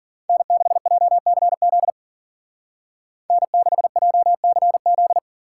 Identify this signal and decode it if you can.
Morse code, heard as a keyed tone, about 49 wpm, 700 Hz. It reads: N6JCZ N6JCZ
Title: Speed Racing - USA 1x3 Call Signs 40wpm